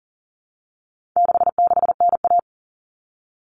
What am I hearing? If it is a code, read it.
Morse code, heard as a keyed tone, about 40 wpm, 700 Hz. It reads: 66NA
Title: All Cut Numbers - 4-Digit 40wpm